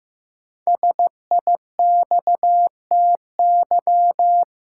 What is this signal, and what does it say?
Morse code, heard as a keyed tone, about 15 wpm, 700 Hz. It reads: SIXTY
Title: Pangram Challenge v2 - 15wpm